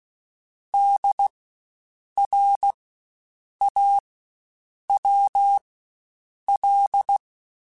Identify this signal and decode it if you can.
Morse code, heard as a keyed tone, about 16 wpm, spaced as for 7 wpm, 800 Hz. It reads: DRAWL
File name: Spaced Alphas / Spaced Alphas mp3s / Ds sp @15